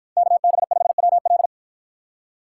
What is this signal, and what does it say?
Morse code, heard as a keyed tone, about 53 wpm, 700 Hz. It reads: KB5WL